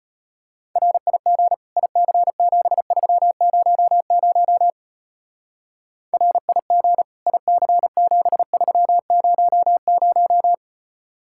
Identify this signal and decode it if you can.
Morse code, heard as a keyed tone, about 31 wpm, 700 Hz. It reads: RIG IC7300 RIG IC7300
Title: Speed-Racing Common QSO Phrases - 25wpm